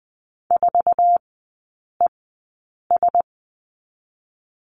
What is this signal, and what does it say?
Morse code, heard as a keyed tone, about 20 wpm, spaced as for 8 wpm, 700 Hz. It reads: 4ES
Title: 067 Lesson 12e - Letter d - Three Characters 20wpm